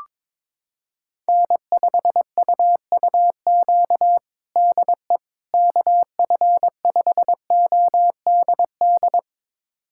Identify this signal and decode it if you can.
Morse code, heard as a keyed tone, about 22 wpm, 700 Hz. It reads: N5UUQ DE KF5ODD
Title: USA Callsign DE Callsign 22wpm